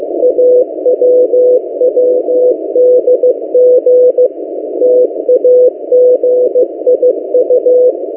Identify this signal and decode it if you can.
Morse code, heard as a keyed tone, about 15 wpm, 500 Hz. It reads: AWWDG TAGIU
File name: M08a- -1355utc-8096khz